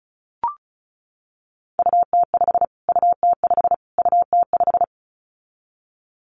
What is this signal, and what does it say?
Morse code, heard as a keyed tone, about 34 wpm, 700 Hz. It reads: UT5 UT5 UT5
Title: Callsign Prefixes for Top 25 DXCC - Repeated 3x - 35wpm